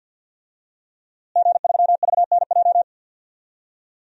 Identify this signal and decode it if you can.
Morse code, heard as a keyed tone, about 50 wpm, 700 Hz. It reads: G3VNJ
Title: DXCC-100 Call Signs - 5 Characters - Repeated 3x - 50wpm